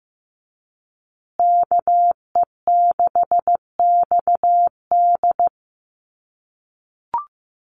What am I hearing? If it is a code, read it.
Morse code, heard as a keyed tone, about 15 wpm, 700 Hz. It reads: KE6XD
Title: USA Call Signs Repeated 3x 15wpm